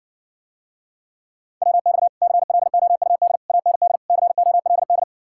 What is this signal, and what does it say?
Morse code, heard as a keyed tone, about 60 wpm, 700 Hz. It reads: WX CLOUD AND COLD